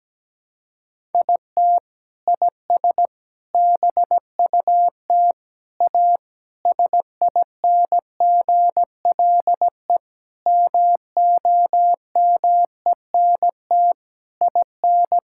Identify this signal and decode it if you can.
Morse code, heard as a keyed tone, about 17 wpm, 700 Hz. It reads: IT IS BUT A SINGLE MOMENT IN
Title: Sentences from Top 1000 Words - Review Part 6 - 17wpm